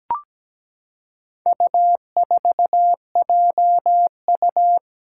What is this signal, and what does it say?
Morse code, heard as a keyed tone, about 17 wpm, 700 Hz. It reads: U4JU
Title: Four Challenging Letter-Number Combinations 17wpm